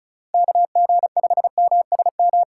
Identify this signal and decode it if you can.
Morse code, heard as a keyed tone, about 36 wpm, 700 Hz. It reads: KG5MSM
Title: USA Call Signs - 6 Characters - 3x Character Spacing - 35wpm